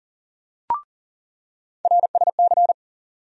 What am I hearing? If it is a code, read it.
Morse code, heard as a keyed tone, about 40 wpm, 700 Hz. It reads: RSC
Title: USA Call Sign Postfixes - 3 Letters 40wpm